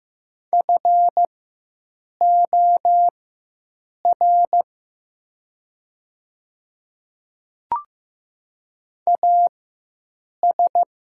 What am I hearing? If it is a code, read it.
Morse code, heard as a keyed tone, about 15 wpm, spaced as for 7 wpm, 700 Hz. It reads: FOR AS
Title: 4x Character Spacing - Top 200 Words - 15wpm